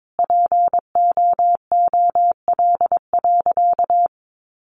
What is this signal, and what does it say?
Morse code, heard as a keyed tone, about 22 wpm, 700 Hz. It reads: POOL.
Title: Additional Easy Phrases 22wpm